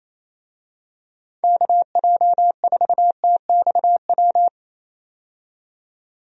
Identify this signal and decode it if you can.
Morse code, heard as a keyed tone, about 27 wpm, 700 Hz. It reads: KJ4TXW
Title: USA Call Signs - 6 Characters - 28wpm